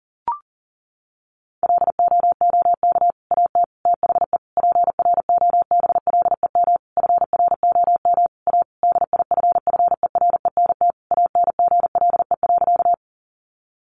Morse code, heard as a keyed tone, about 40 wpm, 700 Hz. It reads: LOOK AT THE PROBLEM FROM A DIFFERENT ANGLE.
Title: Sentences from Top 1500 Words 40wpm